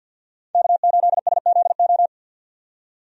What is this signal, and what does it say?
Morse code, heard as a keyed tone, about 46 wpm, 700 Hz. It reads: K8SZO
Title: USA Call Signs - 5 Characters - 50wpm